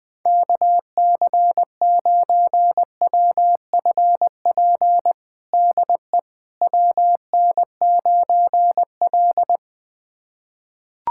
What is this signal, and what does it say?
Morse code, heard as a keyed tone, about 20 wpm, 700 Hz. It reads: KC9WFP DE WN9L
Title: USA Callsign DE Callsign 20wpm